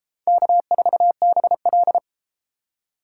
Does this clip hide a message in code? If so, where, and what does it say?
Yes: Morse code from 0.3 to 2.0 s, heard as a keyed tone, about 30 wpm, 700 Hz. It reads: K4BL